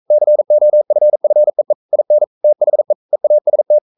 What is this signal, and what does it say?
Morse code, heard as a keyed tone, about 42 wpm, 600 Hz. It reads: COFFEE IN THE EAST
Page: Romanen Never af Ken Follet på engelsk.